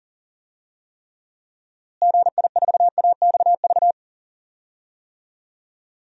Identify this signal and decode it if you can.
Morse code, heard as a keyed tone, about 40 wpm, 700 Hz. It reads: GI4AXV
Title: DXCC-100 Call Signs - 6 Characters - Repeated 3x - 40wpm